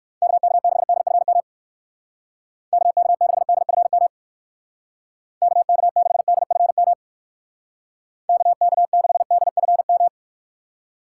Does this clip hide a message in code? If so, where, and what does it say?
Yes: Morse code from 0.2 to 10.1 s, heard as a keyed tone, about 59 wpm, 700 Hz. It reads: KK6DFM KK6DFM KK6DFM KK6DFM